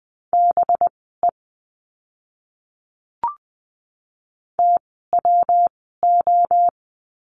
Morse code, heard as a keyed tone, about 20 wpm, spaced as for 15 wpm, 700 Hz. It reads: BE TWO
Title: 192 Lesson 31f - Letter v - Words 20wpm